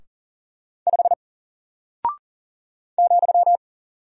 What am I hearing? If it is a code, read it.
Morse code, heard as a keyed tone, about 40 wpm, 700 Hz. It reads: 5 ,
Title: Single Character (Letter:Number:Punctuation:Prosign) Alt 40wpm